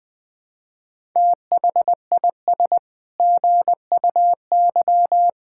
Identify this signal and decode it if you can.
Morse code, heard as a keyed tone, about 21 wpm, 700 Hz. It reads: THIS GUY
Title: Sets of 9 Words - 20wpm